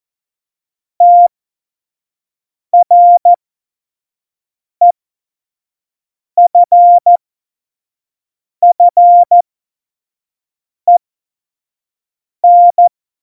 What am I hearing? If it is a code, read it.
Morse code, heard as a keyed tone, about 14 wpm, spaced as for 5 wpm, 700 Hz. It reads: TREFFEN